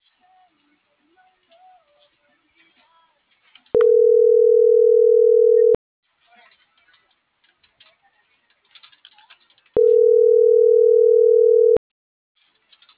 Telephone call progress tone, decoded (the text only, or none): ringback tone